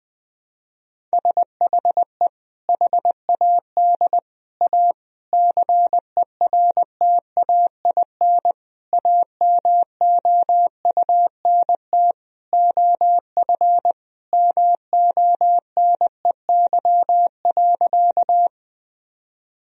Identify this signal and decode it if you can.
Morse code, heard as a keyed tone, about 21 wpm, 700 Hz. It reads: SHE HAD A CERTAIN AMOUNT OF MONEY.
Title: Sentences from Top 1000 Words - Review Part 4 - 20wpm